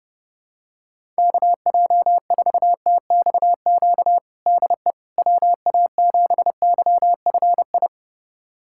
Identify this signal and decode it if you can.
Morse code, heard as a keyed tone, about 29 wpm, 700 Hz. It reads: KJ4TXQ DE WA7YFI
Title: USA Callsign DE Callsign - 6 Characters - 30wpm